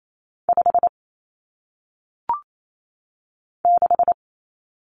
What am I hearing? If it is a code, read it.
Morse code, heard as a keyed tone, about 28 wpm, 700 Hz. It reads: 5 6
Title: Numbers - 1-Digit 28wpm